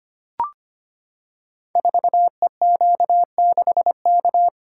Morse code, heard as a keyed tone, about 25 wpm, 700 Hz. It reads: 4EQ6K
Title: Code Groups - 5 Letter:Numbers - 25wpm